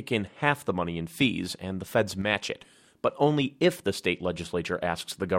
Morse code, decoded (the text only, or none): none